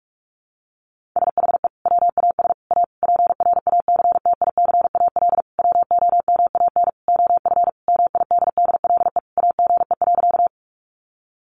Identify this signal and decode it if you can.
Morse code, heard as a keyed tone, about 44 wpm, 700 Hz. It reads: SHE WAS A PRACTICAL WOMAN OF MIDDLE AGE.
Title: Sentences from Top 1400 Words 45wpm